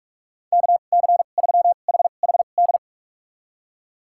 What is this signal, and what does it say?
Morse code, heard as a keyed tone, about 46 wpm, spaced as for 32 wpm, 700 Hz. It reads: KC3HHD
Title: USA Call Signs - 6 Characters - 2x Character Spacing - 45wpm